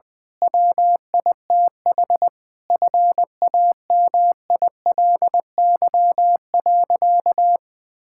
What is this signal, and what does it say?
Morse code, heard as a keyed tone, about 20 wpm, 700 Hz. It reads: WITH FAMILY.
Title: Difficult Sentences with Hints - 20wpm